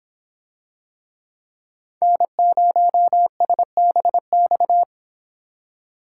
Morse code, heard as a keyed tone, about 24 wpm, 700 Hz. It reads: N0SBX